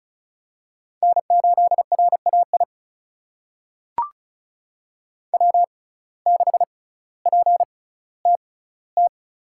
Morse code, heard as a keyed tone, about 36 wpm, spaced as for 18 wpm, 700 Hz. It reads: N8RAI W6PTT
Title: USA Call Signs - 5 Characters - 6x Character Spacing - 35wpm